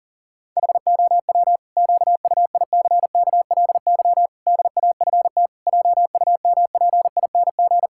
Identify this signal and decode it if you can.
Morse code, heard as a keyed tone, about 40 wpm, 700 Hz. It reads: HOW QUICKLY DAFT JUMPING
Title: Pangram Challenge v2 - 40wpm